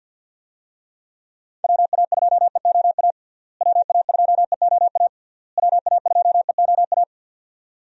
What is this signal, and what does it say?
Morse code, heard as a keyed tone, about 47 wpm, 700 Hz. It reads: WA2EOA WA2EOA WA2EOA
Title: USA Call Signs - 6 Characters - Repeated 3x - 50wpm